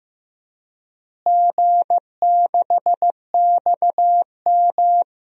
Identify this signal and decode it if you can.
Morse code, heard as a keyed tone, about 16 wpm, 700 Hz. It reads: G6XM
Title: DXCC-100 Call Signs - 4 Characters - Repeated 3x - 15wpm